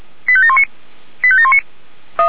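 Five-tone selective call (ZVEI), heard as quick runs of five tones, tones of about 70 ms. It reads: 85719 85719
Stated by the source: ZVEI five-tone selective call received on an SDR.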